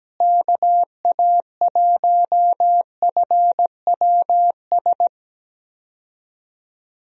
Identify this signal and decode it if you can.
Morse code, heard as a keyed tone, about 17 wpm, 700 Hz. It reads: KA1FWS
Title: USA Call Signs - 6 Characters - Repeated 2x - 17wpm